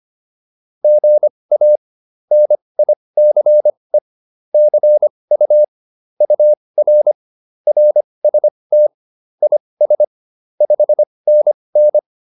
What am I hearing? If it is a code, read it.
Morse code, heard as a keyed tone, about 25 wpm, spaced as for 20 wpm, 600 Hz. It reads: GA NICE CU UR RST IS 5NN
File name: QSO201a 20